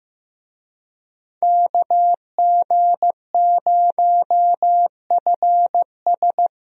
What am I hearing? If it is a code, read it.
Morse code, heard as a keyed tone, about 15 wpm, 700 Hz. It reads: KG0FS
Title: USA Call Signs - 5 Characters - Repeated 2x - 15wpm